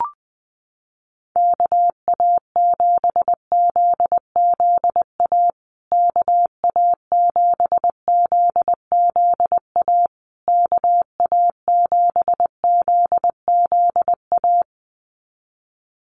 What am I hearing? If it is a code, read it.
Morse code, heard as a keyed tone, about 20 wpm, 700 Hz. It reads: KA7ZZA KA7ZZA KA7ZZA
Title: USA Call Signs Repeated 3x 20wpm